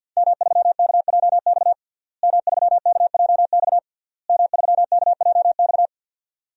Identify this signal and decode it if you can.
Morse code, heard as a keyed tone, about 50 wpm, 700 Hz. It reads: M3KJX M3KJX M3KJX
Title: DXCC-100 Call Signs - 5 Characters - Repeated 3x - 50wpm